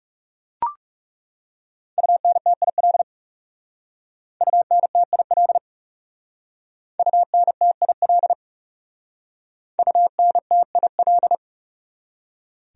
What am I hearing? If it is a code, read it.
Morse code, heard as a keyed tone, about 37 wpm, 700 Hz. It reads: UNTIL UNTIL UNTIL UNTIL
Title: Speed-Racing Top 300 Words 30wpm